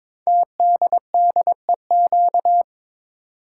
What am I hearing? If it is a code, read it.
Morse code, heard as a keyed tone, about 22 wpm, 700 Hz. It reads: TDDEQ
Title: Code Groups - 5 Letters - 22wpm